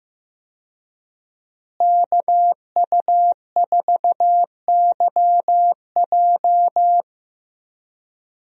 Morse code, heard as a keyed tone, about 15 wpm, 700 Hz. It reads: KU4YJ
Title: USA Call Signs - 5 Characters - Repeated 2x - 15wpm